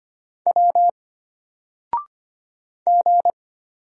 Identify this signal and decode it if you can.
Morse code, heard as a keyed tone, about 25 wpm, 700 Hz. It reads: W G
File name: Single-Letter-Number-25wpm